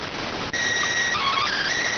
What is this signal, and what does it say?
SSTV (slow-scan television), PD 290.